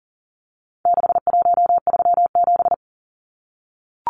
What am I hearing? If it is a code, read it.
Morse code, heard as a keyed tone, about 40 wpm, 700 Hz. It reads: 6137
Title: Numbers - 4-Digit 40wpm